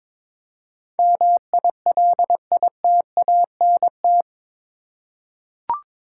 Morse code, heard as a keyed tone, about 22 wpm, 700 Hz. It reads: MILITANT
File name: practice / 22wpm / Eight Letter Words - 22wpm